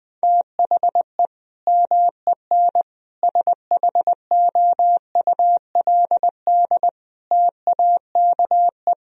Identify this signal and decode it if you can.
Morse code, heard as a keyed tone, about 21 wpm, 700 Hz. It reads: THE MEN SHOULD TAKE
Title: Sentences from Top 1000 Words - Review Part 1 - 20wpm